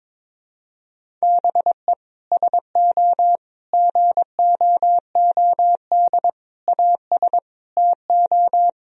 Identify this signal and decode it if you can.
Morse code, heard as a keyed tone, about 22 wpm, 700 Hz. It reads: BE SO GOOD AS TO
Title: Sentences from Top 1000 Words - Review Part 1 - 22wpm